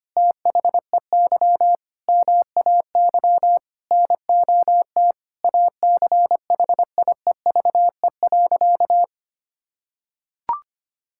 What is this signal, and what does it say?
Morse code, heard as a keyed tone, about 25 wpm, 700 Hz. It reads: THEY MAY NOT ACHIEVE.
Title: Sentences from Top 1000 Words - Review Part 6 - 25wpm